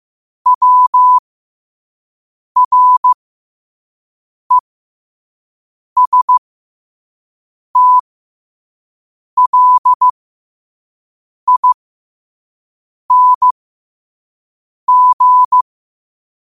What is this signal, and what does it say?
Morse code, heard as a keyed tone, about 15 wpm, spaced as for 5 wpm, 1000 Hz. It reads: WRESTLING
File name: qotdfiles / 5wpm